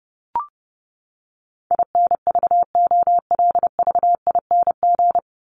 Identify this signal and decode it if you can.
Morse code, heard as a keyed tone, about 30 wpm, 700 Hz. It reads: INVOLVING
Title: ING Suffix - Top 100 Words with Base Words 30wpm